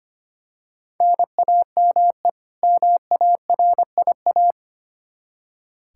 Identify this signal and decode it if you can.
Morse code, heard as a keyed tone, about 25 wpm, 700 Hz. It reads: NAME MARIA
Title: Practice QSO - Exchange 1 - Beginning - 25wpm